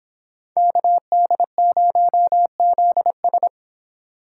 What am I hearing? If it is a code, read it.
Morse code, heard as a keyed tone, about 25 wpm, 700 Hz. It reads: KD0ZS